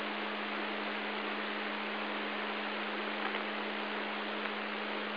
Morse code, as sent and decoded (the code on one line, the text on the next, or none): none